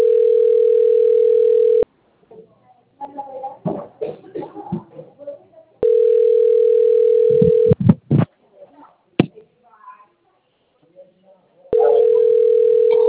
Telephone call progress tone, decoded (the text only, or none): ringback tone